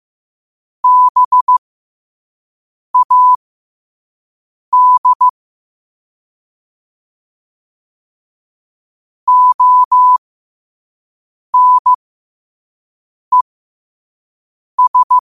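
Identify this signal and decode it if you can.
Morse code, heard as a keyed tone, about 15 wpm, spaced as for 5 wpm, 1000 Hz. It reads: BAD ONES
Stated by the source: Quote-of-the-day Morse code practice at 5 wpm.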